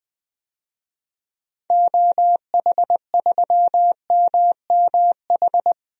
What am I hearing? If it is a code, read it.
Morse code, heard as a keyed tone, about 20 wpm, 700 Hz. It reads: OH3MMH
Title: DXCC-100 Call Signs - 6 Characters - 20wpm